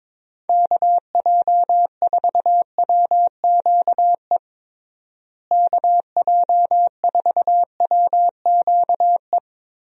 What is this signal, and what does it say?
Morse code, heard as a keyed tone, about 22 wpm, 700 Hz. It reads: KJ4WQE KJ4WQE
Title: USA Call Signs - 6 Characters - Repeated 2x - 22wpm